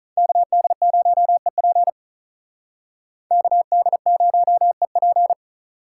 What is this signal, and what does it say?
Morse code, heard as a keyed tone, about 45 wpm, 700 Hz. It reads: KD0EP KD0EP